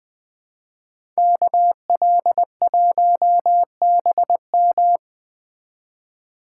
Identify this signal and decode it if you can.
Morse code, heard as a keyed tone, about 20 wpm, 700 Hz. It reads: KL1BM
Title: Speed Racing - USA 2x2 Call Signs 17wpm